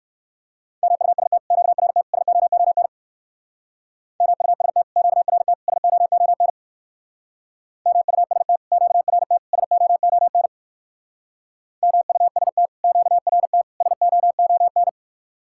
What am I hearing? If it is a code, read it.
Morse code, heard as a keyed tone, about 66 wpm, 700 Hz. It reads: MUST QRT SOON MUST QRT SOON MUST QRT SOON MUST QRT SOON